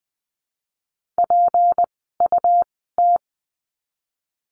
Morse code, heard as a keyed tone, about 20 wpm, spaced as for 14 wpm, 700 Hz. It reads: PUT
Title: 244 Lesson 39f - Letter z - Words 20wpm